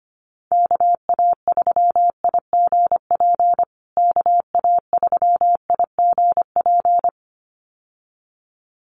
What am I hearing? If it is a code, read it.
Morse code, heard as a keyed tone, about 25 wpm, 700 Hz. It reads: KA3IGP KA3IGP
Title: USA Call Signs Repeated 2x 25wpm